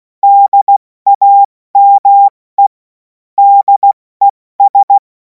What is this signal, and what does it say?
Morse code, heard as a keyed tone, about 16 wpm, 800 Hz. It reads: DAME DES
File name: Morsekurs / Witze / Witz 10. März 2026 70 F80